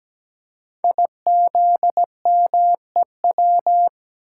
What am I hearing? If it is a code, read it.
Morse code, heard as a keyed tone, about 17 wpm, 700 Hz. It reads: IZMEW
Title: Code Groups - 5 Letters - 17wpm